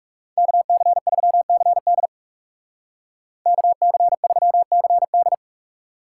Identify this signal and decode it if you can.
Morse code, heard as a keyed tone, about 36 wpm, 700 Hz. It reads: KC3CD KC3CD
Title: Speed Racing - USA Call Signs 30wpm